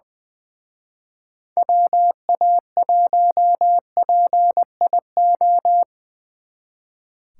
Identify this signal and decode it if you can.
Morse code, heard as a keyed tone, about 25 wpm, 700 Hz. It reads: WA1PIO
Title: Speed Racing - USA 2x3 Call Signs 20wpm